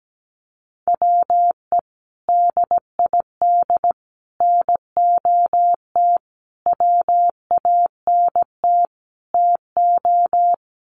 Morse code, heard as a keyed tone, about 17 wpm, 700 Hz. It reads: WE DID NOT WANT TO
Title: Sets of 5 Words Encore 17wpm